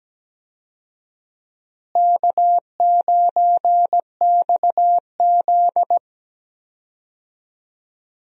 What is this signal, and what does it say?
Morse code, heard as a keyed tone, about 17 wpm, 700 Hz. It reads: K9XZ
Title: USA Call Signs - 4 Characters - Repeated 3x - 17wpm